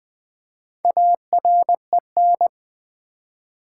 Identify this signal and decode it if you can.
Morse code, heard as a keyed tone, about 20 wpm, 700 Hz. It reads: AREN
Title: Build A Word - 1 Letter At A Time - 3 to 5-Letter Words - 20wpm